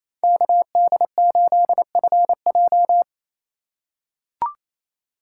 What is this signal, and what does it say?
Morse code, heard as a keyed tone, about 28 wpm, 700 Hz. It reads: KD8FJ
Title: USA Call Signs - 5 Characters - 4x Character Spacing - 28wpm